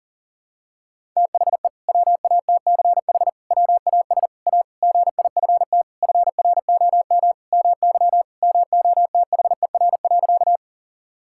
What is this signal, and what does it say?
Morse code, heard as a keyed tone, about 40 wpm, 700 Hz. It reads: THE WATCH WAS A GIFT FROM MY MOTHER.
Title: Sentences from Top 1000 Words - Review Part 6 - 40wpm